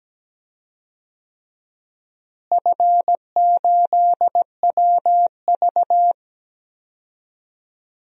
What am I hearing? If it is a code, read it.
Morse code, heard as a keyed tone, about 17 wpm, 700 Hz. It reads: F8WV
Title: DXCC-100 Call Signs - 4 Characters - Repeated 3x - 17wpm